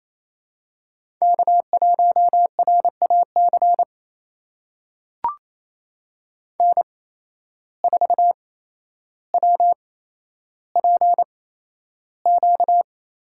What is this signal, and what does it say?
Morse code, heard as a keyed tone, about 28 wpm, spaced as for 12 wpm, 700 Hz. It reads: K1RAC N4WPQ
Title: USA Call Signs - 5 Characters - 8x Character Spacing - 28wpm